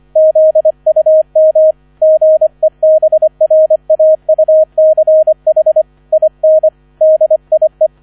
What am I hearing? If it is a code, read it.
Morse code, heard as a keyed tone, about 25 wpm, 630 Hz. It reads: ZUM GEBRAUCH IN DIE